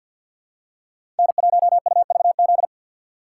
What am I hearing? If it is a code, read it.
Morse code, heard as a keyed tone, about 52 wpm, 700 Hz. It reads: N1UVZ
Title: USA Call Signs - 2x Character Spacing - 50wpm